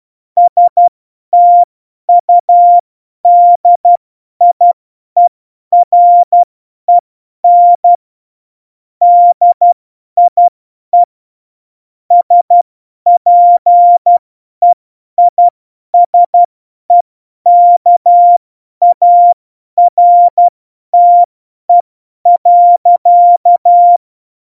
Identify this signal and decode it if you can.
Morse code, heard as a keyed tone, about 12 wpm, spaced as for 10 wpm, 700 Hz. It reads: STUDIEREN DIE SPEISEKARTE.
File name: Morsekurs / Witze / Witz 20. November 2018, 50 F70